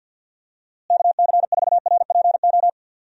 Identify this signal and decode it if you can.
Morse code, heard as a keyed tone, about 52 wpm, 700 Hz. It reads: KC4RPO
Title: USA Call Signs - 6 Characters - Repeated 3x - 50wpm